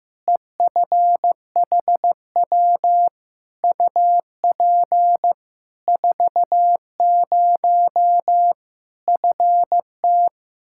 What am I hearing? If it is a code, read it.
Morse code, heard as a keyed tone, about 15 wpm, 700 Hz. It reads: EFHW UP 40 FT
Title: Practice QSO - Exchange 2 - Info - 15wpm